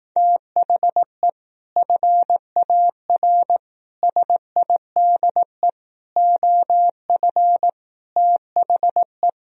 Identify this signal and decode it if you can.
Morse code, heard as a keyed tone, about 18 wpm, 700 Hz. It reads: THE FAR SIDE OF THE
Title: Speed-Racing - Sets of 5 Words 15wpm